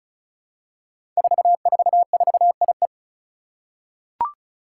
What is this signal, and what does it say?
Morse code, heard as a keyed tone, about 36 wpm, 700 Hz. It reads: 444IE
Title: Five Challenging Letter-Number Combinations 35wpm